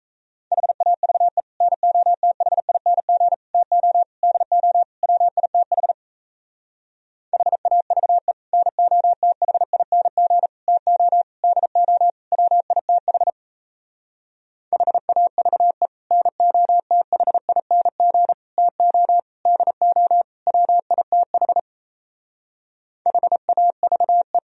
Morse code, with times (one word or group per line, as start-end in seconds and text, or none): 0.5-1.4 HAVE
1.6-3.3 NOTHING
3.5-4.0 TO
4.2-4.8 DO
5.0-5.9 WITH
7.3-8.3 HAVE
8.5-10.5 NOTHING
10.7-11.2 TO
11.4-12.1 DO
12.3-13.3 WITH
14.7-15.9 HAVE
16.1-18.3 NOTHING
18.6-19.2 TO
19.5-20.2 DO
20.5-21.6 WITH
23.1-24.4 HAVE